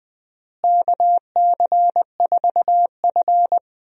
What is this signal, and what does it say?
Morse code, heard as a keyed tone, about 20 wpm, 700 Hz. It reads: KC4F
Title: USA Call Signs - 4 Characters - Repeated 2x - 20wpm